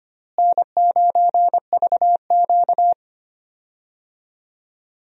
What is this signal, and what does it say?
Morse code, heard as a keyed tone, about 25 wpm, 700 Hz. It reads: N9VQ
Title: USA Call Signs - 4 Characters - Repeated 2x - 25wpm